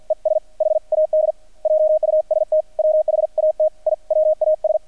Morse code, heard as a keyed tone, about 50 wpm, 620 Hz. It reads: ES HAD JUST WHAT I WAS
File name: archived / CRSnet